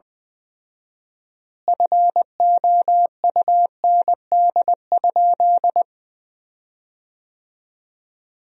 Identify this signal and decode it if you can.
Morse code, heard as a keyed tone, about 20 wpm, 700 Hz. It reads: FOUND?
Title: 4x Word-Spacing - Sentences from Top 300 Words - 20wpm